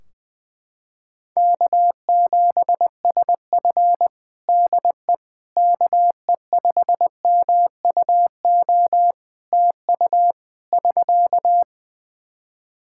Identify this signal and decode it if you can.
Morse code, heard as a keyed tone, about 20 wpm, 700 Hz. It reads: K7SF DE KE5MUO TU <SK>
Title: K1USN SST QSO Practice 20wpm